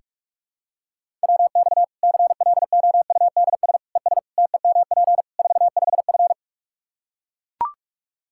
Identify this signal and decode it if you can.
Morse code, heard as a keyed tone, about 42 wpm, 700 Hz. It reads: WX CLOUDS ES TEMP 45F
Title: Practice QSO - Exchange 2 - Info - 45wpm